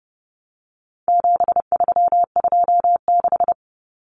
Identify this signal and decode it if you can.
Morse code, heard as a keyed tone, about 30 wpm, 700 Hz. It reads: 7326
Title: Common Cut Numbers - 4-Digit 30wpm